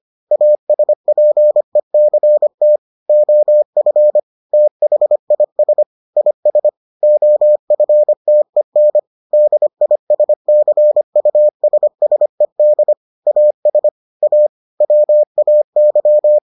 Morse code, heard as a keyed tone, about 25 wpm, 600 Hz. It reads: ASPECT OF THIS IS OFTEN DISCUSSED AS A WAY